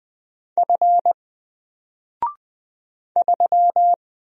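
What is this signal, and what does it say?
Morse code, heard as a keyed tone, about 20 wpm, 700 Hz. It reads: F 3
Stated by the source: Single letter and number Morse drill at 20 wpm, from N4LCD.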